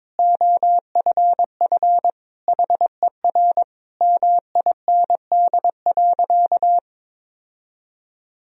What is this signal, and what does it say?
Morse code, heard as a keyed tone, about 22 wpm, 690 Hz. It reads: OFF HER MIND.
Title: Sentences from Top 1000 Words - Review Part 3 - 22wpm